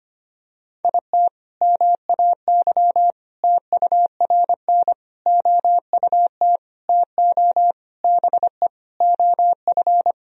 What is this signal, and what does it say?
Morse code, heard as a keyed tone, about 25 wpm, 700 Hz. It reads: IT MAY TURN OUT TO BE OF
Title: Sentences from Top 400 Words and Abbreviations 25wpm